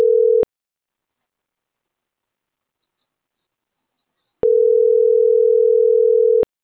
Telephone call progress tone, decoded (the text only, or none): ringback tone